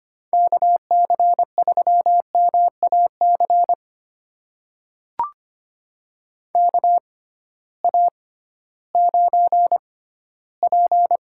Morse code, heard as a keyed tone, about 25 wpm, spaced as for 14 wpm, 700 Hz. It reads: KC3MAC KA9P
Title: USA Call Signs - 6 Characters - 6x Character Spacing - 25wpm